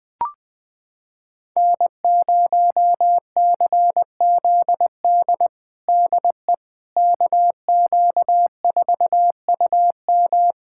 Morse code, heard as a keyed tone, about 20 wpm, 700 Hz. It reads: N0CZD DE KQ4UM
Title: USA Callsign DE Callsign - 5 Characters - 20wpm